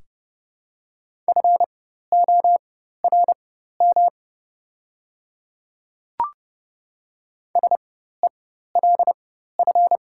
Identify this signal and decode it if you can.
Morse code, heard as a keyed tone, about 31 wpm, spaced as for 14 wpm, 700 Hz. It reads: FORM SELF
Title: 4x Character Spacing - Top 200 Words - 30wpm